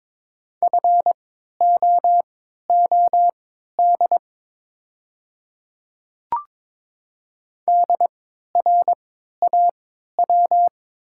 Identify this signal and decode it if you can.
Morse code, heard as a keyed tone, about 22 wpm, spaced as for 12 wpm, 700 Hz. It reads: FOOD DRAW
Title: 3x Character Spacing - Top 300 Words - 22wpm